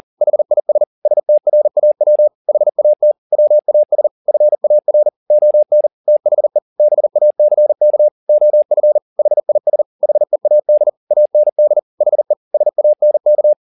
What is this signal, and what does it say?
Morse code, heard as a keyed tone, about 40 wpm, 600 Hz. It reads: HIS STRAW HAT WAS FAR ON THE BACK OF HIS HEAD AND HE SANK
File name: the old man and the sea 2 40wpm